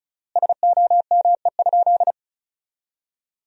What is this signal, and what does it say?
Morse code, heard as a keyed tone, about 36 wpm, 700 Hz. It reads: SOME?
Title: 4x Word-Spacing - Sentences from Top 100 Words - 35wpm